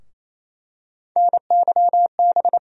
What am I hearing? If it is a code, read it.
Morse code, heard as a keyed tone, about 28 wpm, 700 Hz. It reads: NYB
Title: USA Call Sign Postfixes - 3 Letters 28wpm